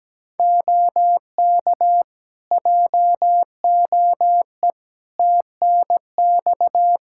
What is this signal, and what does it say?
Morse code, heard as a keyed tone, about 17 wpm, 700 Hz. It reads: OK JOE TNX
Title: Practice QSO - Exchange 4 - Ending - 17wpm